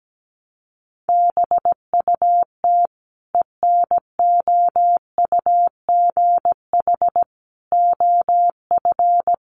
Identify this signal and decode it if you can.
Morse code, heard as a keyed tone, about 17 wpm, 700 Hz. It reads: BUT ENOUGH OF
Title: Sentences from Top 400 Words 17wpm